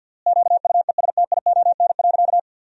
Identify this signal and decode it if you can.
Morse code, heard as a keyed tone, about 50 wpm, 700 Hz. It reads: QUESTION.
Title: 4x Word-Spacing - Sentences from Top 400 Words - 50wpm